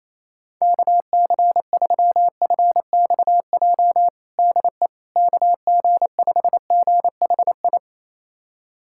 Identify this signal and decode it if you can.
Morse code, heard as a keyed tone, about 28 wpm, 700 Hz. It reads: KC3FXJ DE KG5GHI
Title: USA Callsign DE Callsign 28wpm